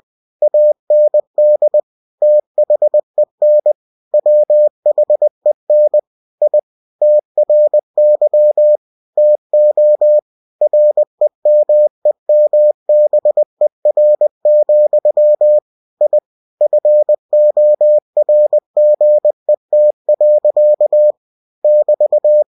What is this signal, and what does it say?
Morse code, heard as a keyed tone, about 20 wpm, 600 Hz. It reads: AND THEN WHEN I TRY TO REMEMBER, I FORGET. <BT>
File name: cw-books / winnie-the-pooh / winnie-20wpm-page-0011